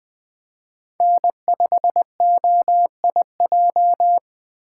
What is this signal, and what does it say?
Morse code, heard as a keyed tone, about 20 wpm, 700 Hz. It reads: N5OIJ
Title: Speed Racing - USA 1x3 Call Signs 17wpm